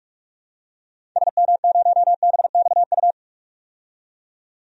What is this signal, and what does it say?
Morse code, heard as a keyed tone, about 46 wpm, 700 Hz. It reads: SM0BXU
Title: DXCC-100 Call Signs - 6 Characters - 45wpm